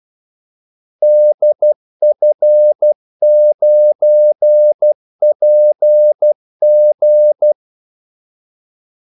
Morse code, heard as a keyed tone, about 12 wpm, 600 Hz. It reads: DF9PG
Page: 2 ekstra ordmellemrum mellem kaldesignalerne